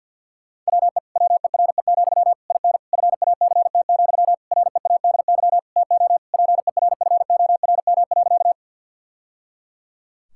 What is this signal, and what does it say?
Morse code, heard as a keyed tone, about 50 wpm, 700 Hz. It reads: WE WERE, IN FACT, READY TO PERFORM.
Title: Sentences from Top 1000 Words - Review Part 5 - 50wpm